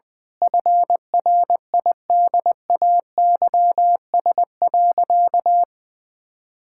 Morse code, heard as a keyed tone, about 20 wpm, 700 Hz. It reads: FRIDAYS.
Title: Difficult Sentences with Hints - 20wpm